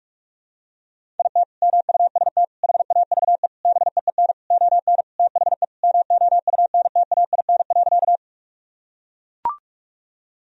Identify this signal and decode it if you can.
Morse code, heard as a keyed tone, about 41 wpm, 700 Hz. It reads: IT MUST HAVE BEEN ON THE MOUNTAIN.
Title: Sentences from Top 1000 Words - Review Part 1 - 45wpm